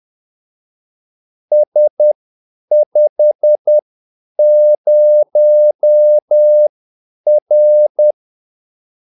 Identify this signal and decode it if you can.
Morse code, heard as a keyed tone, about 10 wpm, spaced as for 8 wpm, 600 Hz. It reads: S50R
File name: Callsigns-10-8wpm